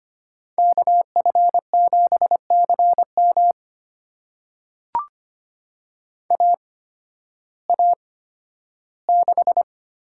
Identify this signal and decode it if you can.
Morse code, heard as a keyed tone, about 25 wpm, spaced as for 13 wpm, 700 Hz. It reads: KF7CM AA6
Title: USA Call Signs - 5 Characters - 8x Character Spacing - 25wpm